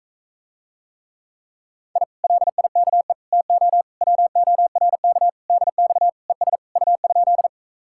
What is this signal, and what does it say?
Morse code, heard as a keyed tone, about 46 wpm, 700 Hz. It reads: I LIKE TO WORK DX ES U?